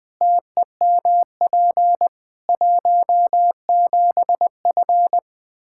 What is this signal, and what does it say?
Morse code, heard as a keyed tone, about 20 wpm, 700 Hz. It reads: TEMP 17F
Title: Practice QSO - Exchange 2 - Info - 20wpm